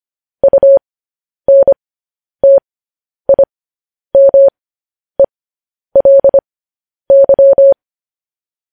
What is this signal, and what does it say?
Morse code, heard as a keyed tone, about 25 wpm, spaced as for 10 wpm, 550 Hz. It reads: UNTIMELY
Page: news headlines in morse code at 10 wpm